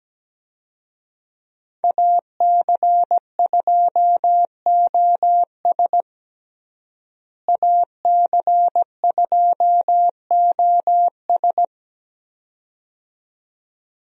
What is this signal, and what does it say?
Morse code, heard as a keyed tone, about 17 wpm, 700 Hz. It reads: AC2OS AC2OS
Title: USA Call Signs - 5 Characters - Repeated 2x - 17wpm